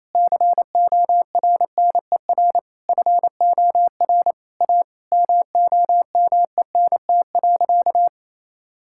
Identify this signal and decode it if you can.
Morse code, heard as a keyed tone, about 28 wpm, 700 Hz. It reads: CORNER FOR A MOMENT.
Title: Sentences from Top 1600 Words 28wpm